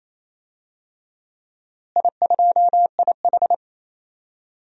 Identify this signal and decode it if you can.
Morse code, heard as a keyed tone, about 28 wpm, 700 Hz. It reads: I2IH
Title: DXCC-100 Call Signs - 4 Characters - Repeated 2x - 28wpm